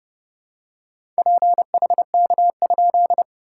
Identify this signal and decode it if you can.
Morse code, heard as a keyed tone, about 30 wpm, 690 Hz. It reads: PHK?
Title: SOTA QSO Practice 30wpm